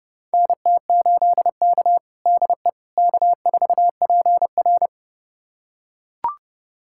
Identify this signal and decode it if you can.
Morse code, heard as a keyed tone, about 32 wpm, 700 Hz. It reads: NT8K DE K4PR
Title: USA Callsign DE Callsign - 4 Characters - 30wpm